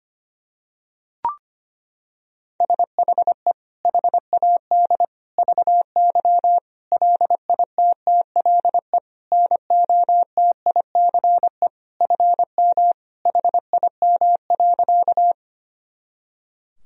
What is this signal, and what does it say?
Morse code, heard as a keyed tone, about 25 wpm, 700 Hz. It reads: SHE HAD VY LITTLE NOTICE FM HIM.
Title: Sentences from Top 400 Words and Abbreviations 25wpm